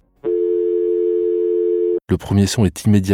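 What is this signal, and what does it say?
Dial tone (North America), 350 + 440 Hz, with speech.
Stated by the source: Dial-up internet modem connection sound.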